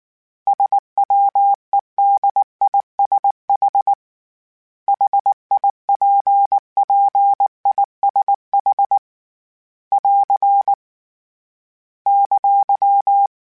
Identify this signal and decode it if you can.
Morse code, heard as a keyed tone, about 19 wpm, 800 Hz. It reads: SWEDISH HIPPISH <AR> !
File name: General / General-mp3s / ISH @18